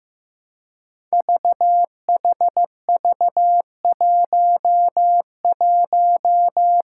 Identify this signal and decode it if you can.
Morse code, heard as a keyed tone, about 15 wpm, 700 Hz. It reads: VHV11
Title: Five Challenging Letter-Number Combinations 15wpm